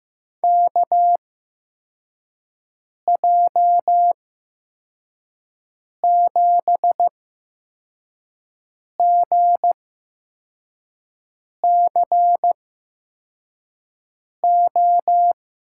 Morse code, heard as a keyed tone, about 15 wpm, spaced as for 4 wpm, 700 Hz. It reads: KJ7GCO
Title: USA Call Signs - 6 Characters - 8x Character Spacing - 15wpm